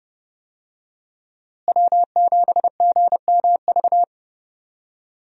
Morse code, heard as a keyed tone, about 30 wpm, 700 Hz. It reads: W7GMV
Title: USA Call Signs - 5 Characters - Repeated 2x - 30wpm